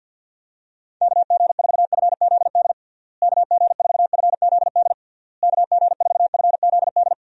Morse code, heard as a keyed tone, about 52 wpm, 700 Hz. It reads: KG4FZD KG4FZD KG4FZD
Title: USA Call Signs - 6 Characters - Repeated 3x - 50wpm